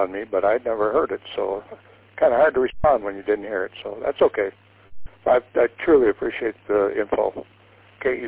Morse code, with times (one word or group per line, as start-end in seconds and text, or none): none